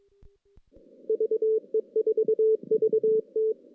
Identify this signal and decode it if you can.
Morse code, heard as a keyed tone, about 23 wpm, 410 Hz. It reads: VE4VT
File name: 2016 JAN CW NAQP / 2016 Jan naqp k5ot - 0-6